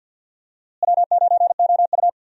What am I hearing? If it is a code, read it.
Morse code, heard as a keyed tone, about 52 wpm, 700 Hz. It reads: W9OU
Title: USA Call Signs - 4 Characters - Repeated 2x - 50wpm